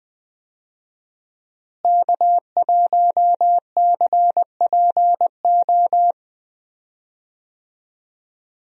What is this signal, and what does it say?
Morse code, heard as a keyed tone, about 20 wpm, 700 Hz. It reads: K1CPO
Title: USA Call Signs - 5 Characters - Repeated 2x - 20wpm